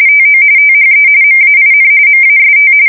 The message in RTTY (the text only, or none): TO THE MAIN MAST